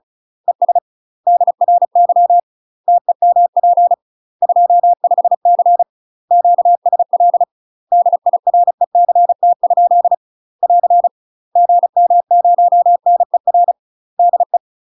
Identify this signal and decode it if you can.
Morse code, heard as a keyed tone, about 36 wpm, 700 Hz. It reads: ES DRY TEMP 25C QSL DIRECT? <AR> GM0NER DE
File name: UK MASTER QSOs TRAINING / 35WPM / UK MASTER QSOs 0014